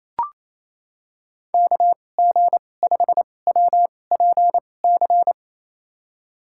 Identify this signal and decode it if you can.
Morse code, heard as a keyed tone, about 28 wpm, spaced as for 20 wpm, 700 Hz. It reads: KG5WPC
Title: USA Call Signs - 6 Characters - 2x Character Spacing - 28wpm